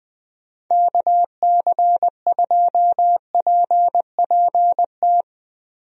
Morse code, heard as a keyed tone, about 20 wpm, 700 Hz. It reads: KC2PPT
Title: USA Call Signs - 6 Characters - 20wpm